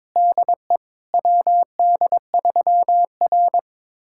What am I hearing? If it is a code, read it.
Morse code, heard as a keyed tone, about 22 wpm, 700 Hz. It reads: DE WD3R
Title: USA Callsign DE Callsign - 4 Characters - 22wpm